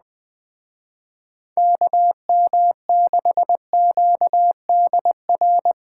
Morse code, heard as a keyed tone, about 21 wpm, 700 Hz. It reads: KM6QDR
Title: Speed Racing - USA 2x3 Call Signs 17wpm